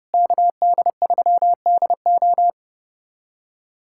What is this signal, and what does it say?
Morse code, heard as a keyed tone, about 30 wpm, 700 Hz. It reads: KD3DO
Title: USA Callsign DE Callsign - 5 Characters - 30wpm